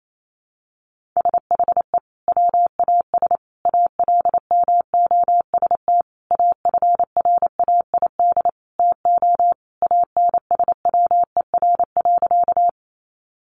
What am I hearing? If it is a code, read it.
Morse code, heard as a keyed tone, about 28 wpm, 700 Hz. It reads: SHE WAS ALMOST AFRAID TO ANSWER.
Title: Sentences from Top 1500 Words 28wpm